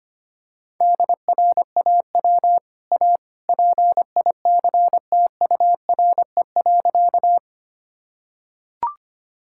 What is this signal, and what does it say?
Morse code, heard as a keyed tone, about 25 wpm, 700 Hz. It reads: DRAW A PICTURE.
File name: Easy Sentences with Hints - 25wpm